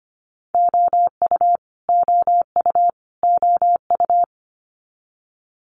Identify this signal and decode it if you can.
Morse code, heard as a keyed tone, about 25 wpm, 700 Hz. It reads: OU OU OU
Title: Multi-letter Phonemes Repeated 3x 25wpm